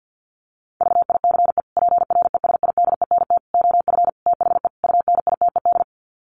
Morse code, heard as a keyed tone, about 50 wpm, 700 Hz. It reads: VICE PRESIDENT OF THE UNITED
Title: Sets of 5 Words Encore2 50wpm